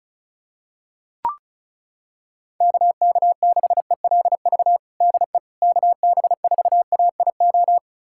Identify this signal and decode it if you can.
Morse code, heard as a keyed tone, about 34 wpm, 700 Hz. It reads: KK6ELV DE KB4AIO
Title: USA Callsign DE Callsign - 6 Characters - 35wpm